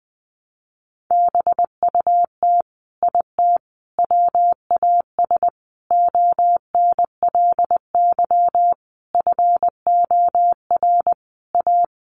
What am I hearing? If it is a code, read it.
Morse code, heard as a keyed tone, about 20 wpm, 700 Hz. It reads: BUT IT WAS ONLY FOR A
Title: Sentences from Top 1000 Words 20wpm